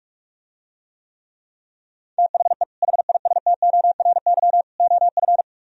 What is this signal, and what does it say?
Morse code, heard as a keyed tone, about 49 wpm, 700 Hz. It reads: THE HISTORY OF